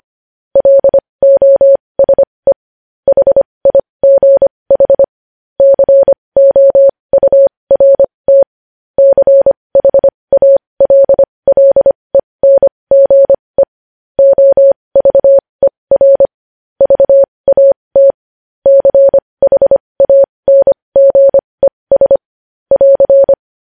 Morse code, heard as a keyed tone, about 25 wpm, spaced as for 20 wpm, 550 Hz. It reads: LOSE HIGH COURT CHALLENGE OVER VAT CHANGES <AR>
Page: news-headlines-in-morse-code-at-20-wpm